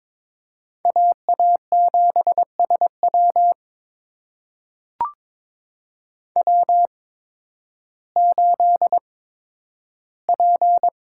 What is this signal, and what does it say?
Morse code, heard as a keyed tone, about 22 wpm, spaced as for 12 wpm, 700 Hz. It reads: AA7SW W8P
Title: USA Call Signs - 5 Characters - 8x Character Spacing - 22wpm